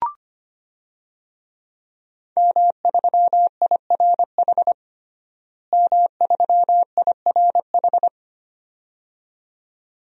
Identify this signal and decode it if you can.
Morse code, heard as a keyed tone, about 25 wpm, 700 Hz. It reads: M3IRH M3IRH
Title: DXCC-100 Call Signs - 5 Characters - Repeated 2x - 25wpm